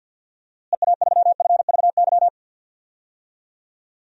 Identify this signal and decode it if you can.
Morse code, heard as a keyed tone, about 52 wpm, 700 Hz. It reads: EA3FVY
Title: DXCC-100 Call Signs - 6 Characters - Repeated 3x - 50wpm